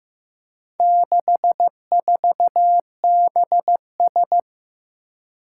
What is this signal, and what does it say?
Morse code, heard as a keyed tone, about 15 wpm, 700 Hz. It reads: 64BS
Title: Four Challenging Letter-Number Combinations 15wpm